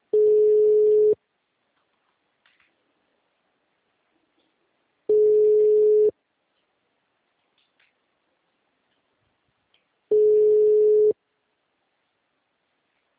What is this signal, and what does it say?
Ringback tone (Europe, 425 Hz).